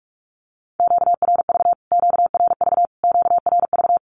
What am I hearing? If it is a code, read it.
Morse code, heard as a keyed tone, about 44 wpm, 700 Hz. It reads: QRV QRV QRV
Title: Q-Codes Repeated 3x 45wpm